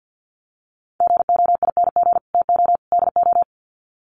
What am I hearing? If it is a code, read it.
Morse code, heard as a keyed tone, about 50 wpm, 700 Hz. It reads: GOING TO DO
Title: Sets of 3 Words 50wpm